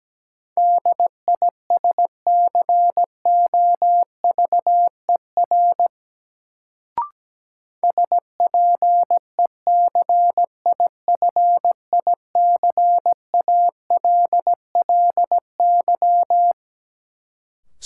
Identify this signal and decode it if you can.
Morse code, heard as a keyed tone, about 17 wpm, 700 Hz. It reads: DISCOVER SPECIFICALLY
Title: Sentences from Top 1100 Words - New Word Drill - 17wpm